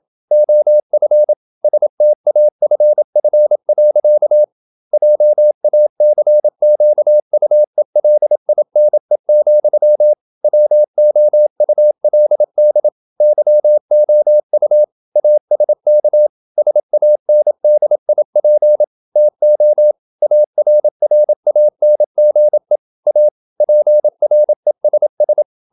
Morse code, heard as a keyed tone, about 27 wpm, 600 Hz. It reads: OF STAFF. JACQUELINE, WOULD YOU ASK SANDIP TO ARRANGE A PRESS
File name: Never 0027wpm